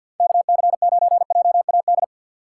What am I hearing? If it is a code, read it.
Morse code, heard as a keyed tone, about 47 wpm, 700 Hz. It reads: KC9JAD
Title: USA Call Signs - 4x Character Spacing - 50wpm